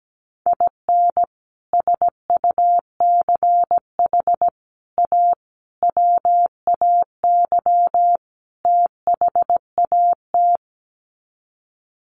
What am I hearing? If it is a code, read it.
Morse code, heard as a keyed tone, about 17 wpm, 700 Hz. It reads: IN SUCH A WAY THAT
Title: Sets of 5 Words 17wpm